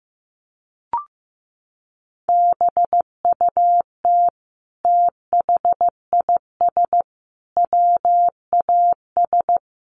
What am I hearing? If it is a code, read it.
Morse code, heard as a keyed tone, about 15 wpm, 700 Hz. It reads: BUT THIS WAS
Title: Sentences from Top 600 Words 15wpm